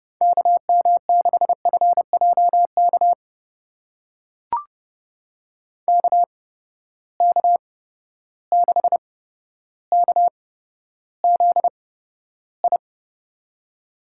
Morse code, heard as a keyed tone, about 30 wpm, spaced as for 13 wpm, 700 Hz. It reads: KM6FJK KK6KZI
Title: USA Call Signs - 6 Characters - 8x Character Spacing - 30wpm